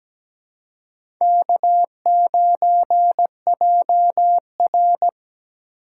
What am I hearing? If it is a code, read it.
Morse code, heard as a keyed tone, about 17 wpm, 700 Hz. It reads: K9JR
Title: USA Call Signs - 4 Characters - 2x Character Spacing - 17wpm